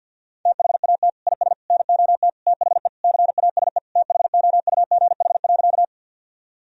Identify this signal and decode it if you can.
Morse code, heard as a keyed tone, about 50 wpm, 700 Hz. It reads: THAT IS NOT THE CASE THOUGH.
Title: Sentences from Top 1000 Words - Review Part 2 - 50wpm